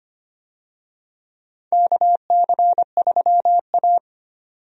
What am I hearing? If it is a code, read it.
Morse code, heard as a keyed tone, about 25 wpm, 700 Hz. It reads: KC3A
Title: USA Call Signs - 4 Characters - 25wpm